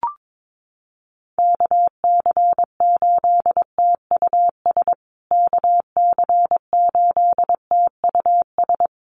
Morse code, heard as a keyed tone, about 22 wpm, 700 Hz. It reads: KC8TUS KC8TUS
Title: USA Call Signs Repeated 2x 22wpm